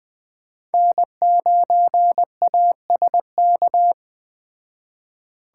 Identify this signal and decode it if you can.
Morse code, heard as a keyed tone, about 20 wpm, 700 Hz. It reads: N9ASK
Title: Speed Racing - USA Call Signs 17wpm